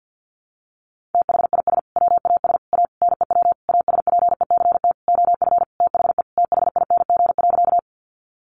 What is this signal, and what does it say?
Morse code, heard as a keyed tone, about 50 wpm, 700 Hz. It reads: THIS WAS A NEW ASPECT OF THE THING.
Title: Sentences from Top 1300 Words 50wpm